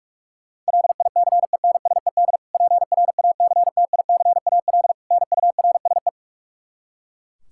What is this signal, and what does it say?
Morse code, heard as a keyed tone, about 46 wpm, 700 Hz. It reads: LICENSED PRACTICAL NURSE
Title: Practice QSO - Exchange 3 - More Info - 45wpm